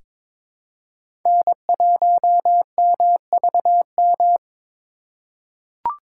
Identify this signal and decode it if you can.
Morse code, heard as a keyed tone, about 22 wpm, 700 Hz. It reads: N1MVM
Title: USA Call Signs - 3x Character Spacing - 22wpm